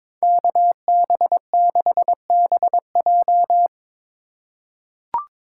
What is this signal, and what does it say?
Morse code, heard as a keyed tone, about 22 wpm, 700 Hz. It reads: KB6BJ
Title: USA Call Signs - 5 Characters - 2x Character Spacing - 22wpm